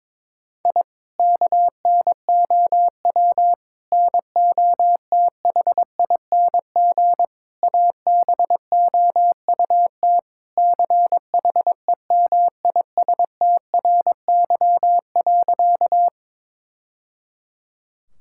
Morse code, heard as a keyed tone, about 22 wpm, 700 Hz. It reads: I KNOW NOTHING ABOUT CHEMISTRY.
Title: Sentences from Top 1000 Words - Review Part 6 - 22wpm